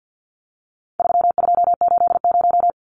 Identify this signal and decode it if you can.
Morse code, heard as a keyed tone, about 50 wpm, 700 Hz. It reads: 3280
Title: Numbers - 4-Digit 50wpm